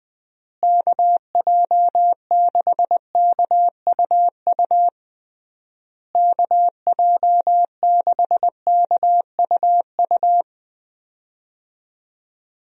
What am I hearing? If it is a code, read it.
Morse code, heard as a keyed tone, about 20 wpm, 700 Hz. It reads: KJ6KUU KJ6KUU
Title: USA Call Signs - 6 Characters - Repeated 3x - 20wpm